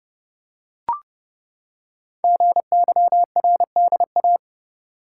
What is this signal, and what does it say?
Morse code, heard as a keyed tone, about 30 wpm, 700 Hz. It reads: GYRDA
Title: Code Groups - 5 Letters - 30wpm